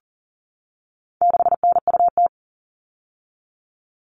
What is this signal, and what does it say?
Morse code, heard as a keyed tone, about 40 wpm, 700 Hz. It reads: 6NUT
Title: All Cut Numbers - 4-Digit 40wpm